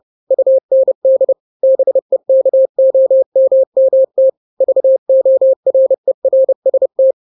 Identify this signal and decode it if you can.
Morse code, heard as a keyed tone, about 30 wpm, 530 Hz. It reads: UND BEKOMMT VORERST
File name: daily / 29wpm